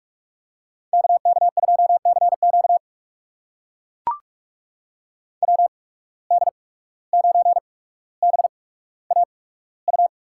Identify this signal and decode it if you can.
Morse code, heard as a keyed tone, about 44 wpm, spaced as for 18 wpm, 700 Hz. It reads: KK2CQ WD9BAU
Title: USA Call Signs - 8x Character Spacing - 45wpm